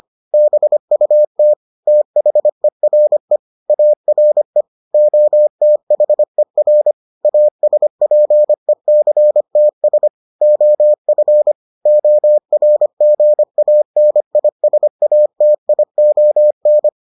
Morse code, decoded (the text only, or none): BUT THERE ARE OTHER ASPECTS OF ORGANISATION